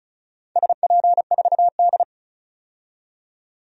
Morse code, heard as a keyed tone, about 36 wpm, 700 Hz. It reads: SP4D
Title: DXCC-100 Call Signs - 4 Characters - 35wpm